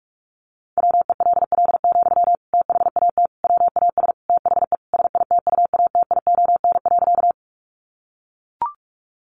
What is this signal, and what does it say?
Morse code, heard as a keyed tone, about 44 wpm, 700 Hz. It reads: WELL, THAT WAS THE SITUATION.
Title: Sentences from Top 1200 Words 45wpm